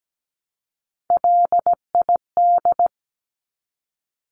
Morse code, heard as a keyed tone, about 17 wpm, 700 Hz. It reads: LID
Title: Abbreviations 17wpm